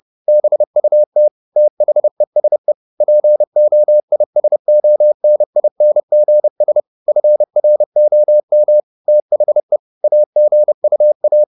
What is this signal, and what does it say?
Morse code, heard as a keyed tone, about 30 wpm, 600 Hz. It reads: BUT THESE POISONINGS FROM THE AGUA
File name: the old man and the sea / the old man and the sea 1 30wpm